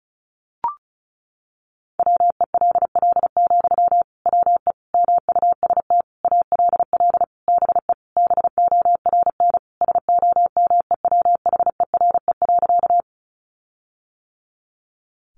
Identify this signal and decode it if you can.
Morse code, heard as a keyed tone, about 34 wpm, 700 Hz. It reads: WELL, WE MUST ALL BE BORN SOMEWHERE.
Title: Sentences from Top 1400 Words 35wpm